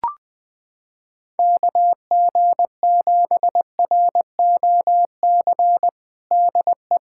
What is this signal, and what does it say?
Morse code, heard as a keyed tone, about 20 wpm, 700 Hz. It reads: KG7ROC DE
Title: USA Callsign DE Callsign - 6 Characters - 20wpm